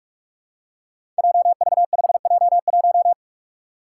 Morse code, heard as a keyed tone, about 44 wpm, 700 Hz. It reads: JV5J1
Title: Five Challenging Letter-Number Combinations 45wpm